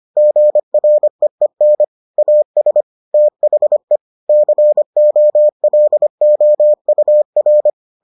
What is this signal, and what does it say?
Morse code, heard as a keyed tone, about 25 wpm, 600 Hz. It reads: GREEN AS THE COLOUR